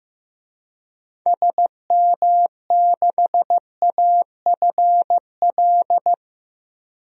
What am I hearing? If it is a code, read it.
Morse code, heard as a keyed tone, about 16 wpm, 700 Hz. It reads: SM6AFL
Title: DXCC-100 Call Signs - 6 Characters - Repeated 3x - 15wpm